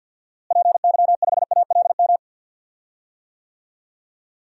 Morse code, heard as a keyed tone, about 53 wpm, 700 Hz. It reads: PY5ALM
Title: DXCC-100 Call Signs - 6 Characters - Repeated 3x - 50wpm